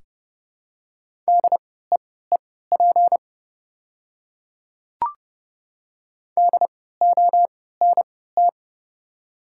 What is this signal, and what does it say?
Morse code, heard as a keyed tone, about 30 wpm, spaced as for 17 wpm, 690 Hz. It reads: DEEP DONT
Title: 3x Character Spacing - Top 500 Words - 30wpm